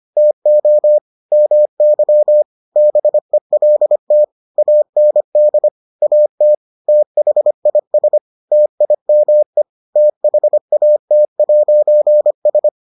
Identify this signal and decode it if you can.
Morse code, heard as a keyed tone, about 25 wpm, 600 Hz. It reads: TO MY BELT AND AT THIS TIME THAT'S